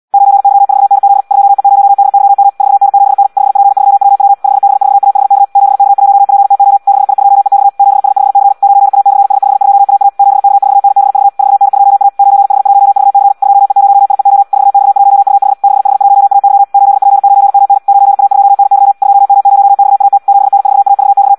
Morse code, heard as a keyed tone, about 74 wpm, 800 Hz. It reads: 9Y4NZ 2E0DYM 4N7N 5P3WW 5B4AHK 7K1PTO 6I2EX 7S5C 9A/S52AA 7K4NBV 4N2A 9H1VC 2E0IEO 4X1UH 6H1AY 8P9MN 9A2NY 2M0XAT 7N4AVX
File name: Audio practica CW / qrq-tx-78 2